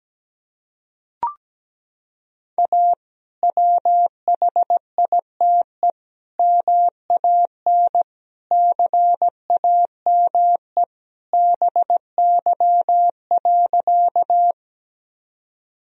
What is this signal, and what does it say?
Morse code, heard as a keyed tone, about 17 wpm, 700 Hz. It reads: A WHITE MAN CAME BY.
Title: Sentences from Top 1000 Words - Review Part 1 - 17wpm